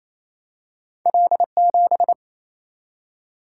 Morse code, heard as a keyed tone, about 28 wpm, 700 Hz. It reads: L7
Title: Single-Letter Single-Number 28wpm